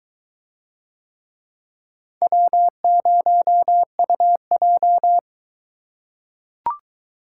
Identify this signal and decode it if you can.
Morse code, heard as a keyed tone, about 22 wpm, 700 Hz. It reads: W0UJ